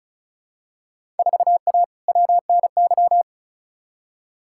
Morse code, heard as a keyed tone, about 34 wpm, 700 Hz. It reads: 4A WNY
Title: Field Day Exchanges Encore - 35wpm